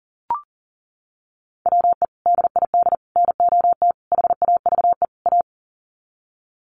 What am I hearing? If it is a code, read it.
Morse code, heard as a keyed tone, about 40 wpm, 700 Hz. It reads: WE DID NOT HAVE A
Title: Sets of 5 Words Encore2 40wpm